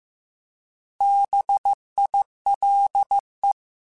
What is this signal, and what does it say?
Morse code, heard as a keyed tone, about 15 wpm, 800 Hz. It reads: BILE